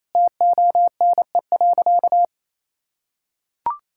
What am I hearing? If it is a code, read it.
Morse code, heard as a keyed tone, about 28 wpm, 700 Hz. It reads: TONE.
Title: Sentences from Top 1600 Words 28wpm